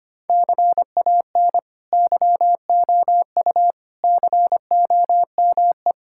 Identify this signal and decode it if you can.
Morse code, heard as a keyed tone, about 25 wpm, 700 Hz. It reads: CAN YOU COME
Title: Sentences from Top 1000 Words - Review Part 5 - 25wpm